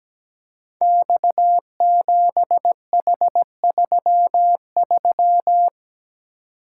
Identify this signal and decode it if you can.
Morse code, heard as a keyed tone, about 17 wpm, 700 Hz. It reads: X7H33
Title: Code Groups - 5 Letter:Numbers - 17wpm